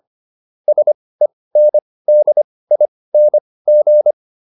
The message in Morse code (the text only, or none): SENDING